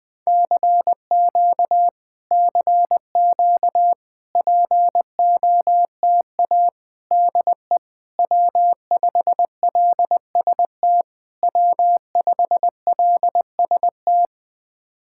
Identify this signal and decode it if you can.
Morse code, heard as a keyed tone, about 21 wpm, 700 Hz. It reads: CQ CQ POTA DE W5LST W5LST
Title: POTA QSO Practice - 20wpm